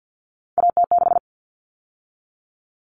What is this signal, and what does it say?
Morse code, heard as a keyed tone, about 50 wpm, 700 Hz. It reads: AT6
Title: Common Cut Numbers - 3-Digit 50wpm